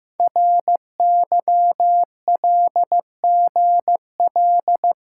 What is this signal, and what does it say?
Morse code, heard as a keyed tone, about 15 wpm, 690 Hz. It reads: RYLGL
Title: Code Groups - 5 Letters - 15wpm